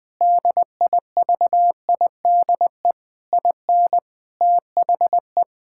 Morse code, heard as a keyed tone, about 21 wpm, 700 Hz. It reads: DIVIDE IN THE
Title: Sentences from Top 1600 Words 20wpm